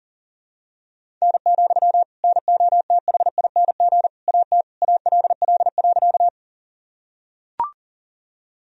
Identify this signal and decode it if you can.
Morse code, heard as a keyed tone, about 37 wpm, 700 Hz. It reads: N, NOTHING AT ALL.
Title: Sentences from Top 500 Words and Abbreviations 40wpm